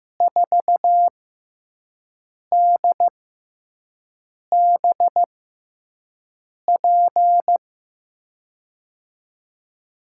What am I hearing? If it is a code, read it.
Morse code, heard as a keyed tone, about 15 wpm, spaced as for 5 wpm, 700 Hz. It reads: 4DBP
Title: USA Call Signs - 6 Characters - 6x Character Spacing - 15wpm